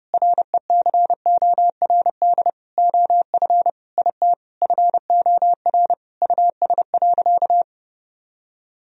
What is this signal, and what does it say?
Morse code, heard as a keyed tone, about 32 wpm, 700 Hz. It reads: RECORD OF IT FOR US.
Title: Sentences from Top 1000 Words - Review Part 3 - 30wpm